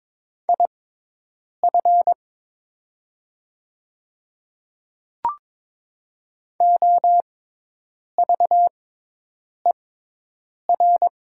Morse code, heard as a keyed tone, about 21 wpm, spaced as for 8 wpm, 700 Hz. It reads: IF OVER
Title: 6x Character Spacing - Top 200 Words - 22wpm